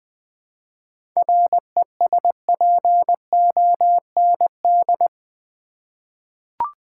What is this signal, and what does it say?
Morse code, heard as a keyed tone, about 19 wpm, 700 Hz. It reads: RESPOND
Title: Seven Letter Words - 20wpm